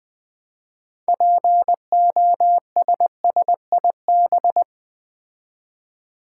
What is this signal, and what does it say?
Morse code, heard as a keyed tone, about 20 wpm, 700 Hz. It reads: POSSIB
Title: Build A Word - 3 Letters At A Time - 8-Letter Words - 20wpm